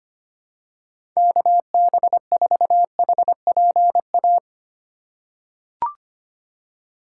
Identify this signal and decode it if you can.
Morse code, heard as a keyed tone, about 25 wpm, 700 Hz. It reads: KB4HPA
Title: USA Call Signs - 4x Character Spacing - 25wpm